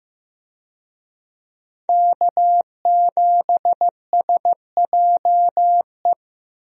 Morse code, heard as a keyed tone, about 15 wpm, 700 Hz. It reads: K7SJE
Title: USA Call Signs - 5 Characters - Repeated 2x - 15wpm